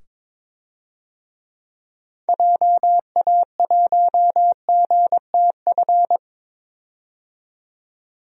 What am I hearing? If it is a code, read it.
Morse code, heard as a keyed tone, about 22 wpm, 700 Hz. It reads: JA1GTF
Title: DXCC-100 Call Signs - 6 Characters - 22wpm